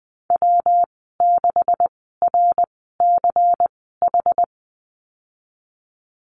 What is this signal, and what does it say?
Morse code, heard as a keyed tone, about 20 wpm, spaced as for 15 wpm, 700 Hz. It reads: W6RCH
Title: 193 Lesson 31g - Letter v - Call Signs 20wpm